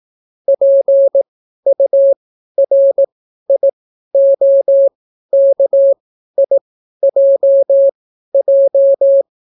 Morse code, heard as a keyed tone, about 18 wpm, spaced as for 12 wpm, 550 Hz. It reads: PURIOKIJJ